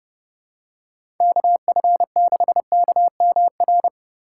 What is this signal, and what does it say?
Morse code, heard as a keyed tone, about 30 wpm, 700 Hz. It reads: KF6KMR
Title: USA Call Signs - 6 Characters - 6x Character Spacing - 30wpm